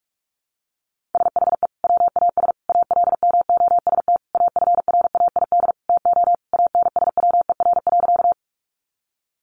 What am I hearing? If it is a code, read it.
Morse code, heard as a keyed tone, about 44 wpm, 700 Hz. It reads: SHE WAS ALMOST AFRAID TO ANSWER.
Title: Sentences from Top 1500 Words 45wpm